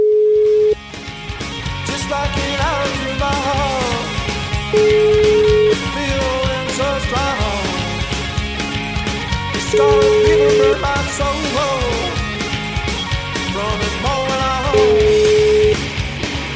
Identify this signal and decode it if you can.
Ringback tone (Europe, 425 Hz), with noise.